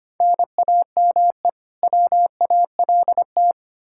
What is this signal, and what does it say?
Morse code, heard as a keyed tone, about 25 wpm, 700 Hz. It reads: NAME WALT
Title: Practice QSO - Exchange 1 - Beginning - 25wpm